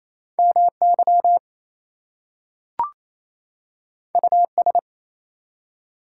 Morse code, heard as a keyed tone, about 27 wpm, 700 Hz. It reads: MY US
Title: Two Letter Words - 28wpm